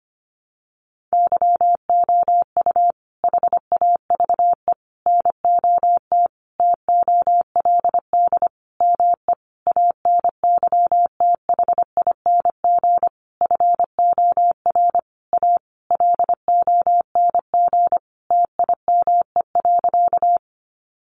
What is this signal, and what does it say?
Morse code, heard as a keyed tone, about 25 wpm, 700 Hz. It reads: YOU HAVE NOT TOLD ME ANYTHING FOR A LONG TIME.
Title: Sentences from Top 1000 Words 25wpm